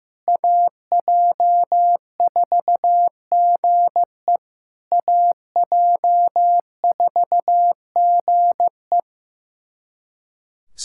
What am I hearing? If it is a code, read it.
Morse code, heard as a keyed tone, about 15 wpm, 700 Hz. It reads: AJ4GE AJ4GE
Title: SOTA QSO Practice 15wpm